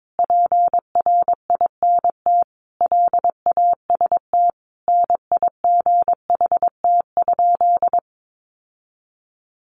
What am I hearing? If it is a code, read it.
Morse code, heard as a keyed tone, about 22 wpm, 700 Hz. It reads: PRINT LAST NIGHT?
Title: Sentences from Top 1400 Words 22wpm